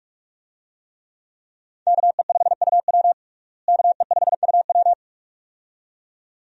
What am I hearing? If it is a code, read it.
Morse code, heard as a keyed tone, about 46 wpm, 700 Hz. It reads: KE5UW KE5UW
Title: USA Call Signs - 5 Characters - Repeated 2x - 45wpm